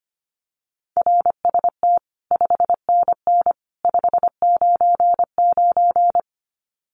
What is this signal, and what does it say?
Morse code, heard as a keyed tone, about 25 wpm, 700 Hz. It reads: RST 5NN 599
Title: Common QSO Phrases 25wpm